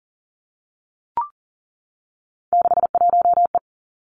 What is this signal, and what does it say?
Morse code, heard as a keyed tone, about 41 wpm, 700 Hz. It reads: 61E
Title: Three Challenging Letter-Number Combinations 40wpm